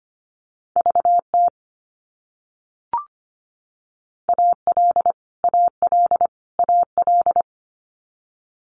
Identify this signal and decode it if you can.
Morse code, heard as a keyed tone, about 25 wpm, 700 Hz. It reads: VT AL AL AL
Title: US States Repeated 3x 25wpm